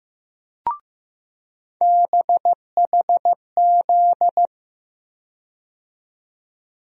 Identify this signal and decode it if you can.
Morse code, heard as a keyed tone, about 15 wpm, 700 Hz. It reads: BHZ
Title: USA Call Sign Postfixes - 3 Letters 15wpm